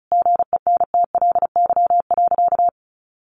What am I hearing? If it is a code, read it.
Morse code, heard as a keyed tone, about 34 wpm, 700 Hz. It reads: GENTLY.
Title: Sentences from Top 1500 Words 35wpm